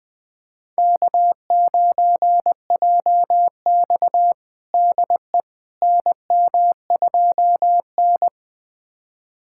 Morse code, heard as a keyed tone, about 20 wpm, 700 Hz. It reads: K9JX DE NM2N
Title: USA Callsign DE Callsign - 4 Characters - 20wpm